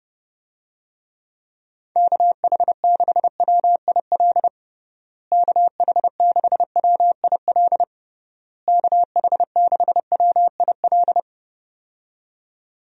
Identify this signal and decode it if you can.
Morse code, heard as a keyed tone, about 32 wpm, 700 Hz. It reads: KH6WIL KH6WIL KH6WIL
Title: USA Call Signs - 6 Characters - Repeated 3x - 30wpm